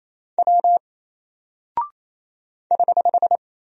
Morse code, heard as a keyed tone, about 28 wpm, 700 Hz. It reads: W <HH>
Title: Single Character (Letter:Number:Punctuation:Prosign) Alt 28wpm